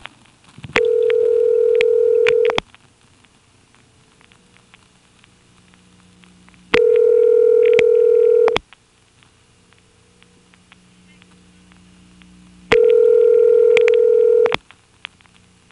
Ringback tone (North America), 440 + 480 Hz.